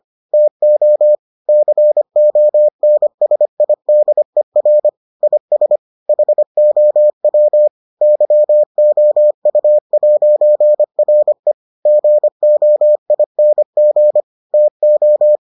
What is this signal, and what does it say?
Morse code, heard as a keyed tone, about 25 wpm, 600 Hz. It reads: TO CONSIDER IS HOW YOU'RE GOING TO